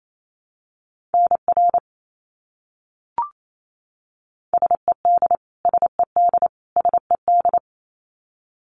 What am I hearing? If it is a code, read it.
Morse code, heard as a keyed tone, about 28 wpm, 700 Hz. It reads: NR SED SED SED
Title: Abbreviations Repeated 3x - 28wpm